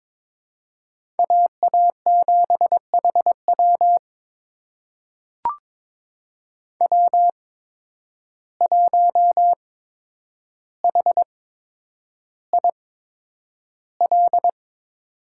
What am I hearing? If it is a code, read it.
Morse code, heard as a keyed tone, about 22 wpm, spaced as for 9 wpm, 700 Hz. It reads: AA7HW W1HIL
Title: USA Call Signs - 5 Characters - 8x Character Spacing - 22wpm